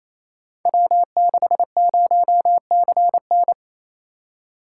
Morse code, heard as a keyed tone, about 28 wpm, 700 Hz. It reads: W60CN
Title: Code Groups - 5 Letter:Numbers - 28wpm